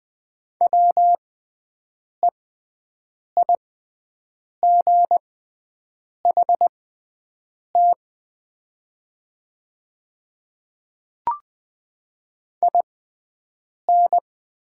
Morse code, heard as a keyed tone, about 20 wpm, spaced as for 7 wpm, 700 Hz. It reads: WEIGHT IN
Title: 6x Character Spacing - Top 500 Words - 20wpm